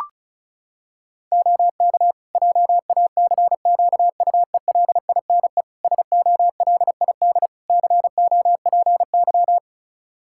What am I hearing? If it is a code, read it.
Morse code, heard as a keyed tone, about 36 wpm, 700 Hz. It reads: OK JACQUELINE SOLID COPY
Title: K1USN SST QSO Practice 35wpm